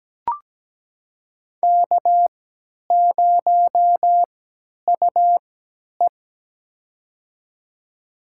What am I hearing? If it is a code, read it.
Morse code, heard as a keyed tone, about 17 wpm, spaced as for 10 wpm, 700 Hz. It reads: K0UE
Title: USA Call Signs - 4 Characters - 3x Character Spacing - 17wpm